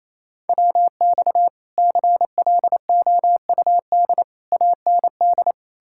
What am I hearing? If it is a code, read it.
Morse code, heard as a keyed tone, about 36 wpm, 700 Hz. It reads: WX CLOUD AND